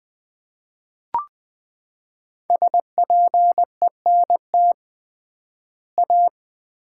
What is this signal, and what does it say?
Morse code, heard as a keyed tone, about 20 wpm, 700 Hz. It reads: SPENT A
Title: 3x Word-Spacing - Sets of 5 Words - 20wpm